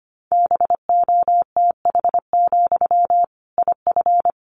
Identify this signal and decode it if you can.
Morse code, heard as a keyed tone, about 25 wpm, 700 Hz. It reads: BOTH, IF
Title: Sentences from Top 1500 Words 25wpm